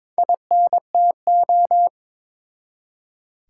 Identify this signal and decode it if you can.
Morse code, heard as a keyed tone, about 22 wpm, 700 Hz. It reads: INTO
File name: practice / 22wpm / Build A Phrase - 2-Word Phrases - 22wpm